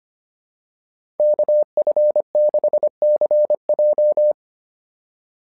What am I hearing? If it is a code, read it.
Morse code, heard as a keyed tone, about 25 wpm, 600 Hz. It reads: KF6CJ
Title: SOTA QSO Practice 25wpm